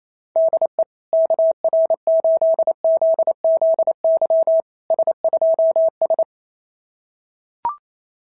Morse code, heard as a keyed tone, about 28 wpm, 650 Hz. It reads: DE KR8ZZY S2S
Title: SOTA QSO Practice 28wpm